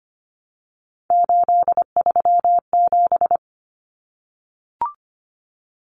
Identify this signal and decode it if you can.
Morse code, heard as a keyed tone, about 25 wpm, 700 Hz. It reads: 837
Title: Numbers - 3-Digit 25wpm